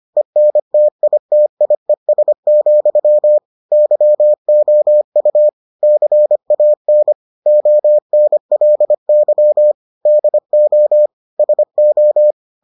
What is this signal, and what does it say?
Morse code, heard as a keyed tone, about 25 wpm, 600 Hz. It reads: ENTITIES, YOU CAN ONLY DO SO